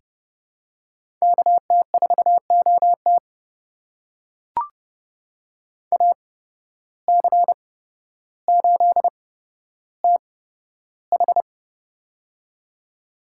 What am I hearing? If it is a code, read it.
Morse code, heard as a keyed tone, about 28 wpm, spaced as for 13 wpm, 700 Hz. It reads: KT4OT AC8TH
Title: USA Call Signs - 5 Characters - 8x Character Spacing - 30wpm